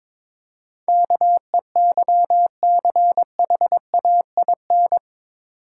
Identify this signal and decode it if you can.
Morse code, heard as a keyed tone, about 22 wpm, 690 Hz. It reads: KEYCHAIN
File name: Easy Words - Speed-Racing with Hints - 22wpm